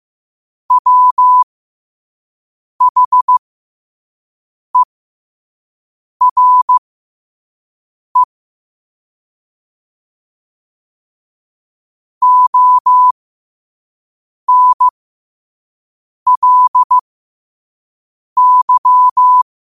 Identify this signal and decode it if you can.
Morse code, heard as a keyed tone, about 15 wpm, spaced as for 5 wpm, 1000 Hz. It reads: WHERE ONLY
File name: qotdfiles / 5wpm